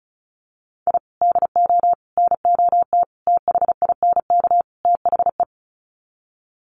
Morse code, heard as a keyed tone, about 34 wpm, 700 Hz. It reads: I DO NOT THINK THE
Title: Sets of 5 Words 35wpm